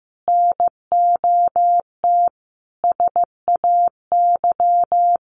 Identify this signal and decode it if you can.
Morse code, heard as a keyed tone, about 15 wpm, 700 Hz. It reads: NOT SAY
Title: Sentences from Top 1100 Words 15wpm